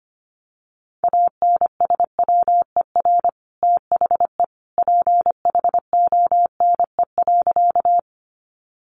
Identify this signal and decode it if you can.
Morse code, heard as a keyed tone, about 25 wpm, 700 Hz. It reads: ANSWER THE PHONE.
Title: Sentences from Top 1100 Words 25wpm